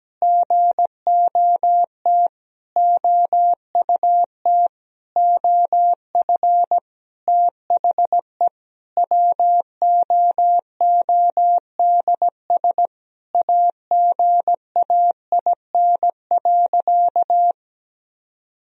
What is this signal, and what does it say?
Morse code, heard as a keyed tone, about 17 wpm, 700 Hz. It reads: GOT OUT OF THE WOODS AGAIN.
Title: Sentences from Top 1000 Words - Review Part 2 - 17wpm